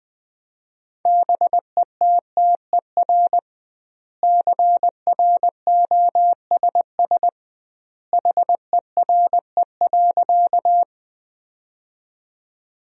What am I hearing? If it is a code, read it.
Morse code, heard as a keyed tone, about 21 wpm, 700 Hz. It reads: BETTER CROSS HERE.
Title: 2x Word-Spacing - Sentences from Top 400 Words - 20wpm